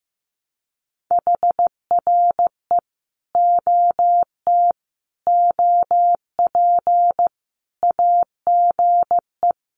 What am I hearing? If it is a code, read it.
Morse code, heard as a keyed tone, about 15 wpm, 700 Hz. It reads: HRE OT OP AGE
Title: Common QSO Phrases 15wpm